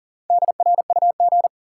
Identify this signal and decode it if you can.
Morse code, heard as a keyed tone, about 40 wpm, 700 Hz. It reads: DRUG
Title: Sets of 5 Words - New Word Drill - 40wpm